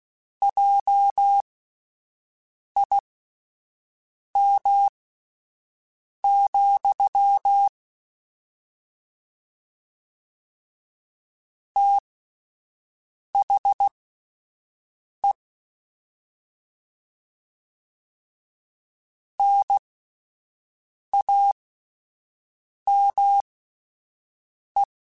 Morse code, heard as a keyed tone, about 16 wpm, spaced as for 5 wpm, 800 Hz. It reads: JIM, THE NAME
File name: QSOs / QSOs-mp3s / WD0BGP 1@13